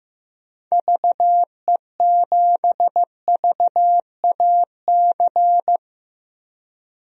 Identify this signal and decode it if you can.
Morse code, heard as a keyed tone, about 15 wpm, 700 Hz. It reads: VE7VAC
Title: DXCC-100 Call Signs - 6 Characters - 15wpm